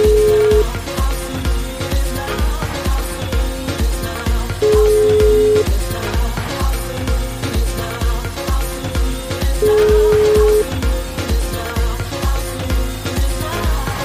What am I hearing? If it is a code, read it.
Ringback tone (Europe, 425 Hz), with noise.